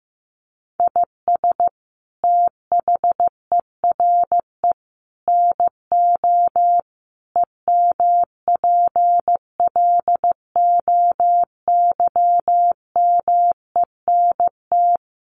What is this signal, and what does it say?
Morse code, heard as a keyed tone, about 15 wpm, 700 Hz. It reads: IS THERE NO EMPLOYMENT
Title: Sentences from Top 1500 Words 15wpm